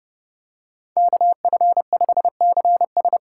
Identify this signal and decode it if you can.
Morse code, heard as a keyed tone, about 31 wpm, 700 Hz. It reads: KF5CS
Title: USA Call Signs - 3x Character Spacing - 30wpm